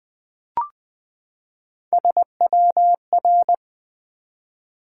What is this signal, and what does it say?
Morse code, heard as a keyed tone, about 20 wpm, 700 Hz. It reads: SWR
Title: Top 180 QSO Elements 20wpm